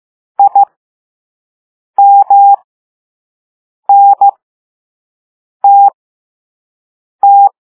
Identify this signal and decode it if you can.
Morse code, heard as a keyed tone, about 15 wpm, spaced as for 5 wpm, 800 Hz. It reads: IMNTT